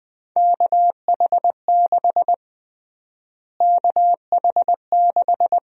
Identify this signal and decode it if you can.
Morse code, heard as a keyed tone, about 20 wpm, 700 Hz. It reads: KH6 KH6
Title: USA Call Sign Prefixes - Repeated 2x - 20wpm